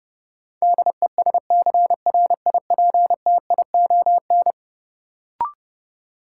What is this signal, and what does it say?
Morse code, heard as a keyed tone, about 32 wpm, 700 Hz. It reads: DESCRIPTION
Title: Sentences from Top 1300 Words - New Word Drill Alt - 30wpm